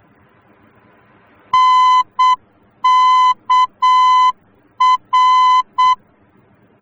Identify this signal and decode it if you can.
Morse code, heard as a keyed tone, about 7 wpm, 1010 Hz. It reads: NKR